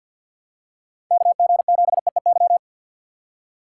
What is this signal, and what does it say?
Morse code, heard as a keyed tone, about 52 wpm, 700 Hz. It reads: KG7EEY
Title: USA Call Signs - 3x Character Spacing - 50wpm